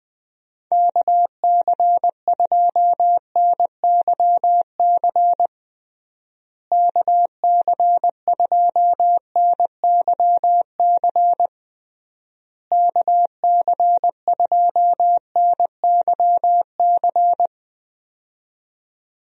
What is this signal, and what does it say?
Morse code, heard as a keyed tone, about 20 wpm, 700 Hz. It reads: KC2NYC KC2NYC KC2NYC
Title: USA Call Signs - 6 Characters - Repeated 3x - 20wpm